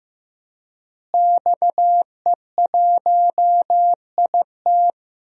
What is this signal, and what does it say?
Morse code, heard as a keyed tone, about 15 wpm, 700 Hz. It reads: XE1IT
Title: DXCC-100 Call Signs - 5 Characters - Repeated 2x - 15wpm